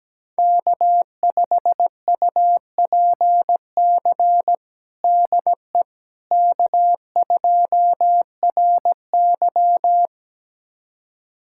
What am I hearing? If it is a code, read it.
Morse code, heard as a keyed tone, about 17 wpm, 700 Hz. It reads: K5UPC DE K2RY
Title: USA Callsign DE Callsign 17wpm